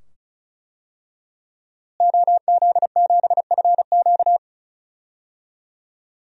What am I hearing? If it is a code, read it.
Morse code, heard as a keyed tone, about 34 wpm, 700 Hz. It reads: OZ7FQ
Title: DXCC-100 Call Signs - 5 Characters - Repeated 2x - 35wpm